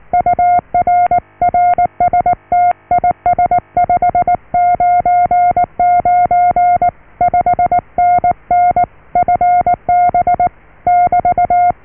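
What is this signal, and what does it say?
Morse code, heard as a keyed tone, about 19 wpm, 710 Hz. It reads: UR RST IS 599 5NN FB <BT>
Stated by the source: Morse QSO on 14 MHz with a German station, received at RST 579.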